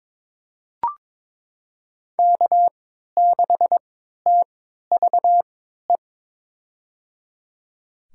Morse code, heard as a keyed tone, about 22 wpm, spaced as for 13 wpm, 700 Hz. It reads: K6TVE
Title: USA Call Signs - 3x Character Spacing - 22wpm